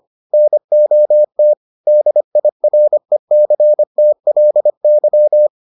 Morse code, heard as a keyed tone, about 25 wpm, 600 Hz. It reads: NOT DIRECTLY